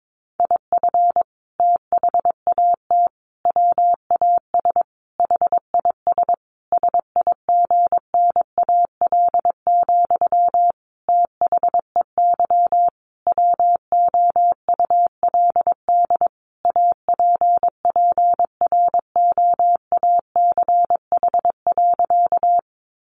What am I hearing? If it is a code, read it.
Morse code, heard as a keyed tone, about 22 wpm, 700 Hz. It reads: IF THAT WAS HIS SIGNAL, THEY WOULD APPROACH.
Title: Sentences from Top 1400 Words 22wpm